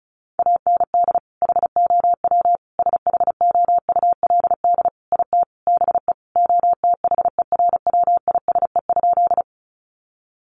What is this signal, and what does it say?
Morse code, heard as a keyed tone, about 34 wpm, 700 Hz. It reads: AND HOW SHOULD IT BE OTHERWISE?
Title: Sentences from Top 1200 Words 35wpm